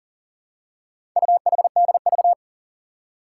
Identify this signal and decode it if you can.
Morse code, heard as a keyed tone, about 42 wpm, 700 Hz. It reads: UHDV
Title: Four Challenging Letter-Number Combinations 40wpm